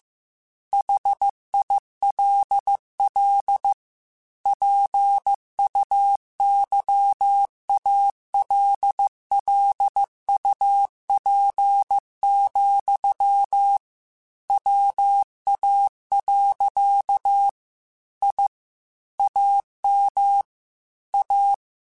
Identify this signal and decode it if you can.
Morse code, heard as a keyed tone, about 15 wpm, 800 Hz. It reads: HILL PUYALLUP, WA. I AM A
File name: QSOs / QSOs-mp3s / kc7sac@13